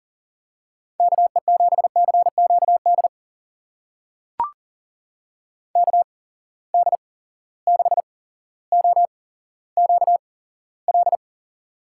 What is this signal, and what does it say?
Morse code, heard as a keyed tone, about 41 wpm, spaced as for 17 wpm, 700 Hz. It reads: KE7CQD KD6OQL
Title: USA Call Signs - 6 Characters - 8x Character Spacing - 40wpm